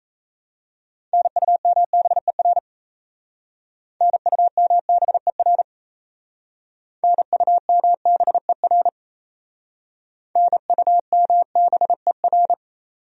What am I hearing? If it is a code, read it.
Morse code, heard as a keyed tone, about 36 wpm, 700 Hz. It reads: NUMBER NUMBER NUMBER NUMBER